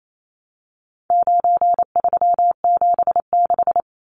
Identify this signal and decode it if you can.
Morse code, heard as a keyed tone, about 28 wpm, 700 Hz. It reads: 9376
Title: Numbers - 4-Digit 28wpm